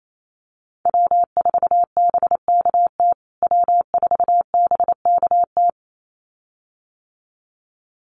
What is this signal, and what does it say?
Morse code, heard as a keyed tone, about 28 wpm, 700 Hz. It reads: W4BKT W4BKT
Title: USA Call Signs Repeated 2x 28wpm